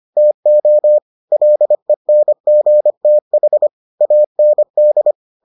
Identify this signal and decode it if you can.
Morse code, heard as a keyed tone, about 25 wpm, 600 Hz. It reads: TO LENGTH AND